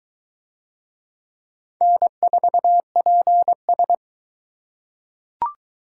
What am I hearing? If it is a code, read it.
Morse code, heard as a keyed tone, about 22 wpm, 700 Hz. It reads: N4PS